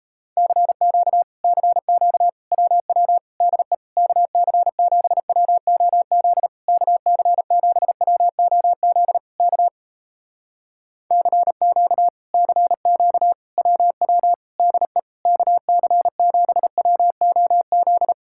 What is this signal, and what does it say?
Morse code, heard as a keyed tone, about 38 wpm, 700 Hz. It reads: CQ CQ WW DE KC7WOZ KC7WOZ K CQ CQ WW DE KC7WOZ